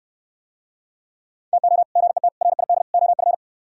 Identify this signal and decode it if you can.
Morse code, heard as a keyed tone, about 68 wpm, 700 Hz. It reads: TO GET RID OF